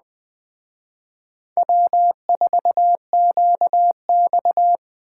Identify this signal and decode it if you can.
Morse code, heard as a keyed tone, about 20 wpm, 700 Hz. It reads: W4QX